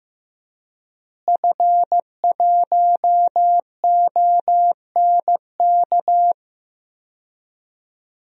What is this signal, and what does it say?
Morse code, heard as a keyed tone, about 15 wpm, 700 Hz. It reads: F1ONK
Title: DXCC-100 Call Signs - 5 Characters - Repeated 3x - 15wpm